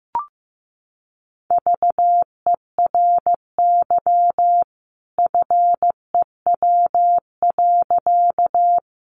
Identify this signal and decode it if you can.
Morse code, heard as a keyed tone, about 15 wpm, 700 Hz. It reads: VERY FEW.
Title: A Sound Of Thunder 15wpm